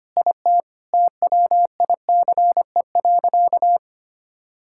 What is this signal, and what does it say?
Morse code, heard as a keyed tone, about 25 wpm, 700 Hz. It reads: IT TWICE.
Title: Sentences from Top 1600 Words 25wpm